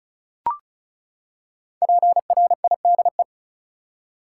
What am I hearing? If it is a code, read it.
Morse code, heard as a keyed tone, about 36 wpm, 700 Hz. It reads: PRIDE
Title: Five Letter Words - 35wpm